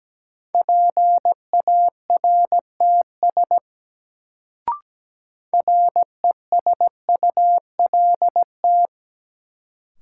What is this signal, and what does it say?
Morse code, heard as a keyed tone, about 17 wpm, 700 Hz. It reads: PARTS RESULT
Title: Sets of 3 Words - New Word Drill - 17wpm